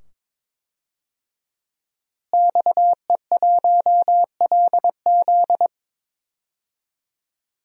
Morse code, heard as a keyed tone, about 22 wpm, 700 Hz. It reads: XE1LZ
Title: DXCC-100 Call Signs - 5 Characters - Repeated 2x - 22wpm